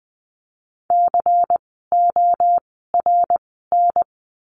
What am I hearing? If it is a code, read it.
Morse code, heard as a keyed tone, about 20 wpm, spaced as for 15 wpm, 700 Hz. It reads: CORN
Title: 094 Lesson 16f - Letter c - Words 20wpm